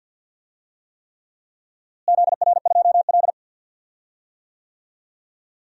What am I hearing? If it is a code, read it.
Morse code, heard as a keyed tone, about 52 wpm, 700 Hz. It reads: ZR2L
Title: DXCC-100 Call Signs - 4 Characters - Repeated 3x - 50wpm